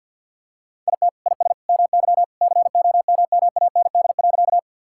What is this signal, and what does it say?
Morse code, heard as a keyed tone, about 50 wpm, 700 Hz. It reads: IT IS MY COMMAND.
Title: Sentences from Top 1600 Words 50wpm